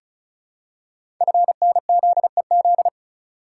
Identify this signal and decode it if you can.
Morse code, heard as a keyed tone, about 36 wpm, 700 Hz. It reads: FNZEZ
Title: Code Groups - 5 Letters - 35wpm